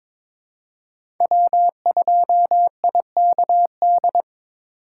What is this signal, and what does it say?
Morse code, heard as a keyed tone, about 23 wpm, 700 Hz. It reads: W2IKD
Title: USA Call Signs - 4x Character Spacing - 22wpm